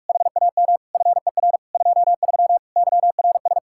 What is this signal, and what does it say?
Morse code, heard as a keyed tone, about 59 wpm, spaced as for 52 wpm, 700 Hz. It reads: HAM FER 23 YRS
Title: Speed-Racing Common QSO Phrases - 45wpm